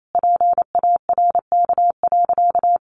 Morse code, heard as a keyed tone, about 28 wpm, 700 Hz. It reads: PARK.
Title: Sentences from Top 1400 Words 28wpm